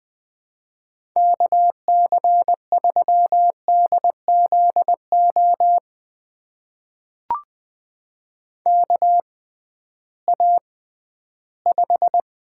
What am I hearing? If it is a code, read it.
Morse code, heard as a keyed tone, about 20 wpm, spaced as for 13 wpm, 700 Hz. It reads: KC3DZO KA5
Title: USA Call Signs - 6 Characters - 6x Character Spacing - 20wpm